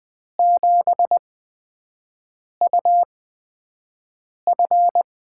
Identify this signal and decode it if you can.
Morse code, heard as a keyed tone, about 21 wpm, spaced as for 5 wpm, 700 Hz. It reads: 7UF
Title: USA Call Signs - 5 Characters - 8x Character Spacing - 20wpm